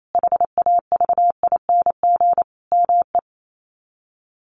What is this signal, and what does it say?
Morse code, heard as a keyed tone, about 28 wpm, 700 Hz. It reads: HAVING ME
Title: Sets of 5 Words Encore 28wpm